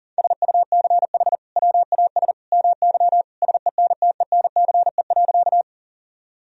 Morse code, heard as a keyed tone, about 38 wpm, 700 Hz. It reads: SUCH WAS MY SENTENCE.
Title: Sentences from Top 1000 Words - Review Part 1 - 40wpm